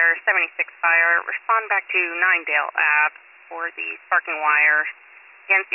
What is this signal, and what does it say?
Emergency Alert System (SAME) tones (header bursts), heard as harsh buzzing bursts, with speech. It is heard from 0.8 to 5.0 s.